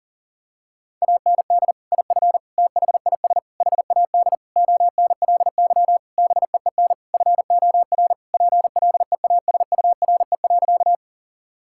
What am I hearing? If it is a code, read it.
Morse code, heard as a keyed tone, about 40 wpm, 700 Hz. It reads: AND IF THIS HAD ONLY BEEN FOR PLEASURE.
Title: Sentences from Top 1600 Words 40wpm